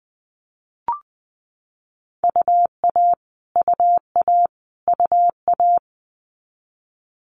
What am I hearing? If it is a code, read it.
Morse code, heard as a keyed tone, about 20 wpm, 700 Hz. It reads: UA UA UA
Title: Multi-letter Phonemes Repeated 3x 20wpm